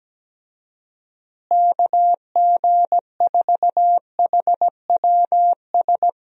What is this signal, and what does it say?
Morse code, heard as a keyed tone, about 17 wpm, 700 Hz. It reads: KG4HWS
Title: USA Call Signs - 6 Characters - Repeated 2x - 17wpm